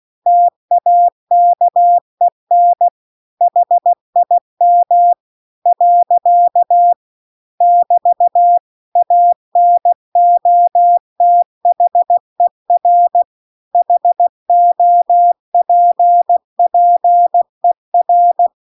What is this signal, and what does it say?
Morse code, heard as a keyed tone, about 16 wpm, 700 Hz. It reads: TAKEN HIM . <BT> ANOTHER HOPPER